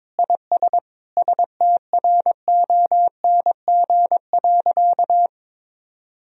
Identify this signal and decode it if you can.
Morse code, heard as a keyed tone, about 21 wpm, 700 Hz. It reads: IS STRONG.
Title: Sentences from Top 1000 Words - Review Part 2 - 22wpm